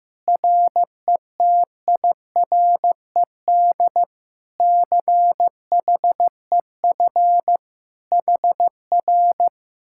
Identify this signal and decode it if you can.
Morse code, heard as a keyed tone, about 15 wpm, 700 Hz. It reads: RETIRED CHEF HR
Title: Speed-Racing Common QSO Phrases Encore - 15wpm